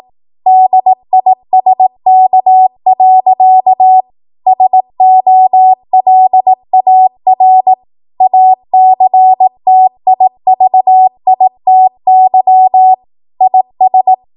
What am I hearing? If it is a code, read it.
Morse code, heard as a keyed tone, about 18 wpm, 750 Hz. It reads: DISK. SOLAR ACTIVITY IS
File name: Morse / 18WPM